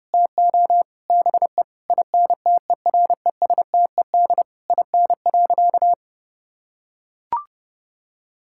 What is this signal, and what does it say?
Morse code, heard as a keyed tone, about 30 wpm, 700 Hz. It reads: TO BE INTERESTED IN.
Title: Sentences from Top 1000 Words - Review Part 6 - 30wpm